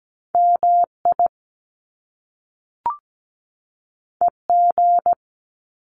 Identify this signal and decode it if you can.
Morse code, heard as a keyed tone, about 17 wpm, 700 Hz. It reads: MI EG
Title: Top Two Letter Combinations in Words 17wpm